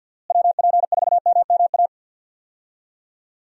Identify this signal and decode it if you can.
Morse code, heard as a keyed tone, about 47 wpm, 700 Hz. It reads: WP4MMA
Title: USA Call Signs - 6 Characters - Repeated 2x - 50wpm